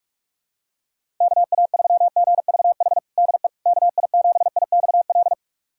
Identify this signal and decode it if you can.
Morse code, heard as a keyed tone, about 45 wpm, 700 Hz. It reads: KA3GVH DE KI7IXL
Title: USA Callsign DE Callsign - 6 Characters - 45wpm